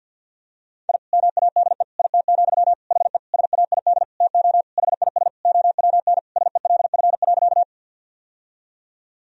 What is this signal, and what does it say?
Morse code, heard as a keyed tone, about 51 wpm, 700 Hz. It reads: I MADE IT, HE SAID TO HIS OWN SELF.
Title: Sentences from Top 1000 Words - Review Part 1 - 50wpm